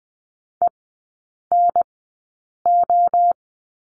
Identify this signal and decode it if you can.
Morse code, heard as a keyed tone, about 20 wpm, spaced as for 8 wpm, 700 Hz. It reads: ENO
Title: 041 Lesson 8e - Number 1 - Three Characters 20wpm